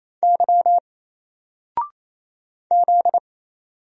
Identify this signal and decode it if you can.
Morse code, heard as a keyed tone, about 27 wpm, 700 Hz. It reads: Y Z
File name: Single-Letter-Number-28wpm